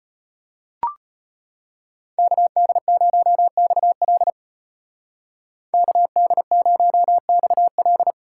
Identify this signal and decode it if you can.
Morse code, heard as a keyed tone, about 35 wpm, 700 Hz. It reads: KD0XL KD0XL